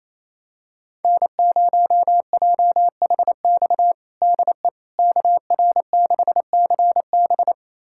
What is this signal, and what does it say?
Morse code, heard as a keyed tone, about 27 wpm, 700 Hz. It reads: N0JHX DE KR6CB
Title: USA Callsign DE Callsign - 5 Characters - 28wpm